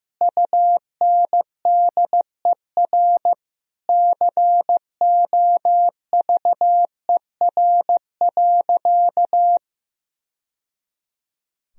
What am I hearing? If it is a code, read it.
Morse code, heard as a keyed tone, about 15 wpm, 700 Hz. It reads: UNDER COVER.
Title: Sentences from Top 1000 Words - Review Part 1 - 15wpm